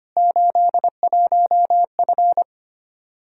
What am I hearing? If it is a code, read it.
Morse code, heard as a keyed tone, about 25 wpm, 700 Hz. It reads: 81F
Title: Practice QSO - Exchange 2 - Info - 25wpm